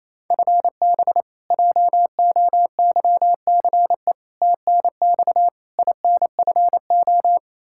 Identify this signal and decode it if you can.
Morse code, heard as a keyed tone, about 28 wpm, 700 Hz. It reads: FB JOYCE TNX INFO
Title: K1USN SST QSO Practice 28wpm